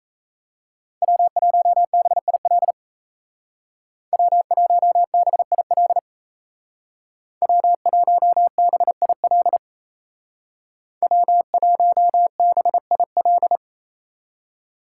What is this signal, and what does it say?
Morse code, heard as a keyed tone, about 37 wpm, 700 Hz. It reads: W1BIL W1BIL W1BIL W1BIL